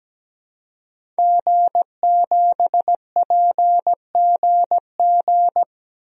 Morse code, heard as a keyed tone, about 17 wpm, 700 Hz. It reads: G7PGG
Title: DXCC-100 Call Signs - 5 Characters - 17wpm